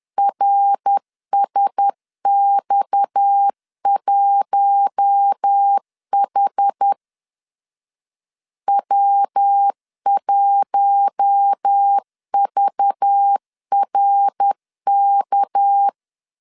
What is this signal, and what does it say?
Morse code, heard as a keyed tone, about 10 wpm, 800 Hz. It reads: RSX1H W1VRK
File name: LEZIONI / 6 PKX0 / 6 PKX 10wpm 2426